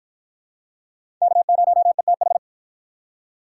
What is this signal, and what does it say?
Morse code, heard as a keyed tone, about 47 wpm, 700 Hz. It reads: K0ETH